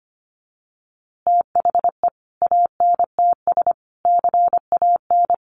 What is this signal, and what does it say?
Morse code, heard as a keyed tone, about 25 wpm, 700 Hz. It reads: THE ANTS CAN
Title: Easy Phrases 25wpm